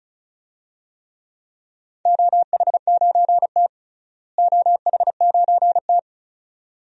Morse code, heard as a keyed tone, about 36 wpm, 700 Hz. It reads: OH9T OH9T
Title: DXCC-100 Call Signs - 4 Characters - Repeated 2x - 35wpm